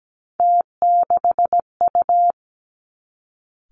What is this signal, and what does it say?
Morse code, heard as a keyed tone, about 17 wpm, 700 Hz. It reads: T6U
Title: Three Challenging Letter-Number Combinations 17wpm